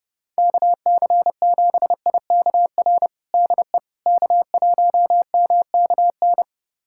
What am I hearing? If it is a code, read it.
Morse code, heard as a keyed tone, about 30 wpm, 700 Hz. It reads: KC7IKR DE K1MKN
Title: USA Callsign DE Callsign 30wpm